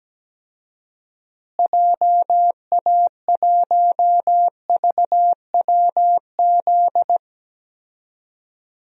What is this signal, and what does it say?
Morse code, heard as a keyed tone, about 17 wpm, 700 Hz. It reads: JA1VWZ
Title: DXCC-100 Call Signs - 6 Characters - Repeated 3x - 17wpm